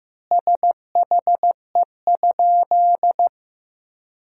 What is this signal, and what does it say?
Morse code, heard as a keyed tone, about 15 wpm, 700 Hz. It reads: SHE?
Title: 2x Word-Spacing - Sentences from Top 100 Words - 15wpm